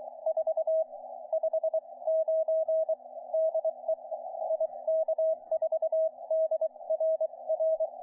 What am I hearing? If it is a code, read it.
Morse code, heard as a keyed tone, about 23 wpm, spaced as for 20 wpm, 660 Hz. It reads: 4 59 DE IK4DRR
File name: VK3HJ-160-CW-29MAR16